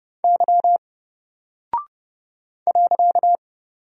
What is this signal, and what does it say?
Morse code, heard as a keyed tone, about 28 wpm, 700 Hz. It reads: Y .
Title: Single Character (Letter:Number:Punctuation:Prosign) 30wpm